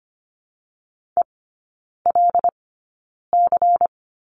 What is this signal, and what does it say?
Morse code, heard as a keyed tone, about 25 wpm, spaced as for 9 wpm, 700 Hz. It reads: ELC
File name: course / 25:10wpm / 093 Lesson 16e - Letter c - Three Characters 25wpm